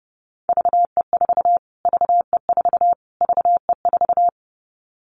Morse code, heard as a keyed tone, about 30 wpm, 700 Hz. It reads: VE4 VE4 VE4
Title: Callsign Prefixes for Top 25 DXCC - Repeated 3x - 30wpm